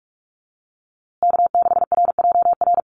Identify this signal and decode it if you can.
Morse code, heard as a keyed tone, about 44 wpm, 700 Hz. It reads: K6RJR
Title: USA Call Signs Repeated 2x 45wpm